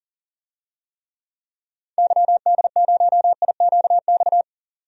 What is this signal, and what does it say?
Morse code, heard as a keyed tone, about 40 wpm, 700 Hz. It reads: YD0IQX
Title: DXCC-100 Call Signs - 6 Characters - 40wpm